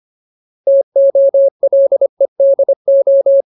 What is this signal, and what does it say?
Morse code, heard as a keyed tone, about 25 wpm, 550 Hz. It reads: TOLEDO
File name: Ultimate Word List / 25WPM Ultimate